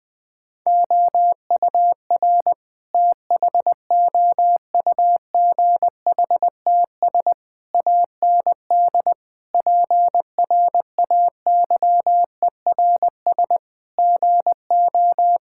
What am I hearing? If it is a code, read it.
Morse code, heard as a keyed tone, about 20 wpm, 700 Hz. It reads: OUR THOUGHTS AND PRAYERS GO
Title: Sets of 8 Words - 20wpm